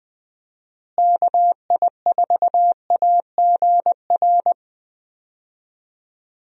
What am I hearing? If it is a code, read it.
Morse code, heard as a keyed tone, about 21 wpm, 700 Hz. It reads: KI4AGR
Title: USA Call Signs - 6 Characters - Repeated 2x - 20wpm